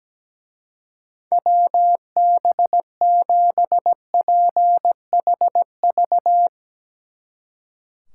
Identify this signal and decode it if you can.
Morse code, heard as a keyed tone, about 17 wpm, 700 Hz. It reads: WB7PHV
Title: Speed Racing - USA 2x3 Call Signs 17wpm